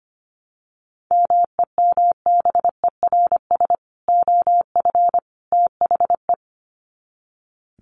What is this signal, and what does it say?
Morse code, heard as a keyed tone, about 25 wpm, 700 Hz. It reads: MEMBERS OF THE
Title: Sets of 3 Words 25wpm